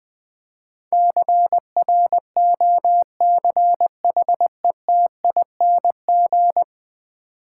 Morse code, heard as a keyed tone, about 20 wpm, 700 Hz. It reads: CROCHETING
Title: Easy Words - Speed-Racing with Hints - 20wpm